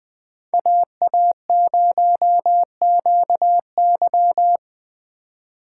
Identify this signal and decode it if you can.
Morse code, heard as a keyed tone, about 20 wpm, 700 Hz. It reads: AA0QY